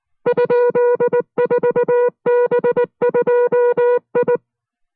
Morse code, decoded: ?4B2I